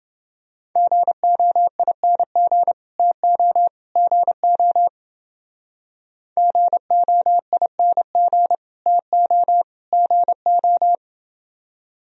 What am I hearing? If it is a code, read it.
Morse code, heard as a keyed tone, about 29 wpm, 700 Hz. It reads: GOING TO GO GOING TO GO